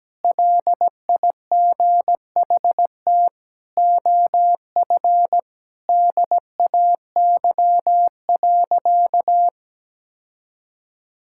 Morse code, heard as a keyed tone, about 17 wpm, 700 Hz. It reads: LIGHT OF DAY.
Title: Sentences from Top 1000 Words - Review Part 5 - 17wpm